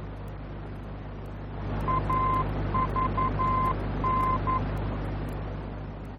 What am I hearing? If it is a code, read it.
Morse code, heard as a keyed tone, about 11 wpm, 1040 Hz, with noise. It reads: AVN
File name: france / mx / AVN-F-VOR